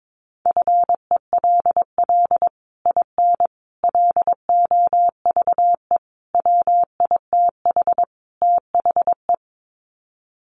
Morse code, heard as a keyed tone, about 22 wpm, 700 Hz. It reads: FELL IN LOVE WITH THE
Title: Sets of 5 Words Encore2 22wpm